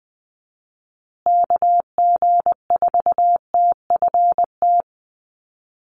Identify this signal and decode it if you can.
Morse code, heard as a keyed tone, about 20 wpm, 700 Hz. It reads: KG4TFT
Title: USA Call Signs Repeated 2x 20wpm